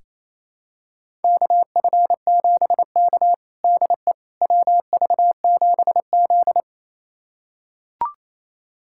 Morse code, heard as a keyed tone, about 28 wpm, 700 Hz. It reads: KF7K DE WV7Z
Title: USA Callsign DE Callsign - 4 Characters - 28wpm